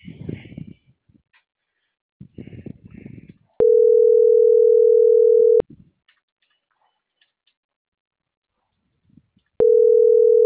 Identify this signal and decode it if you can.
Ringback tone (North America), 440 + 480 Hz.